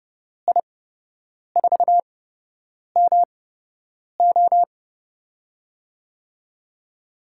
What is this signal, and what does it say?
Morse code, heard as a keyed tone, about 29 wpm, spaced as for 8 wpm, 700 Hz. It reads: I4MO
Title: USA Call Signs - 5 Characters - 8x Character Spacing - 30wpm